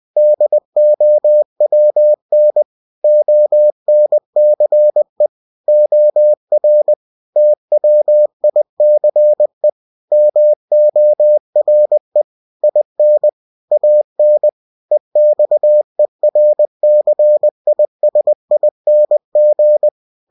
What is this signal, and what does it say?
Morse code, heard as a keyed tone, about 20 wpm, 600 Hz. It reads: DOWN ONCE OR TWICE MORE IN AN EXERCISING